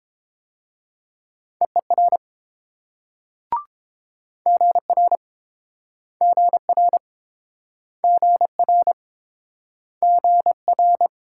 Morse code, heard as a keyed tone, about 31 wpm, 700 Hz. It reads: EER GR GR GR GR